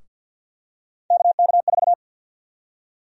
Morse code, heard as a keyed tone, about 48 wpm, 700 Hz. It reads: KK4
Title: USA Call Sign Prefixes - 50wpm